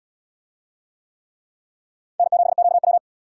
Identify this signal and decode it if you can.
Morse code, heard as a keyed tone, about 68 wpm, 700 Hz. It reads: N7QW